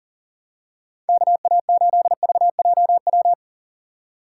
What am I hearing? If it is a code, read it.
Morse code, heard as a keyed tone, about 38 wpm, 700 Hz. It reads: KA8VJW
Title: USA Call Signs - 6 Characters - 2x Character Spacing - 40wpm